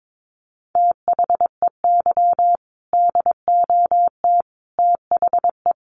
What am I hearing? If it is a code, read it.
Morse code, heard as a keyed tone, about 22 wpm, 700 Hz. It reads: THEY DOT THE
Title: Sentences from Top 1500 Words 22wpm